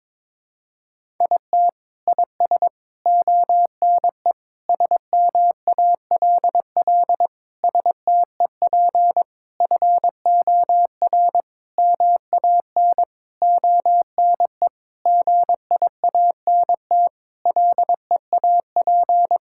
Morse code, heard as a keyed tone, about 22 wpm, 700 Hz. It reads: IT IS ONE SMALL STEP FOR MAN ONE GIANT LEAP
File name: Sentences with 9-Word Phrases - 22wpm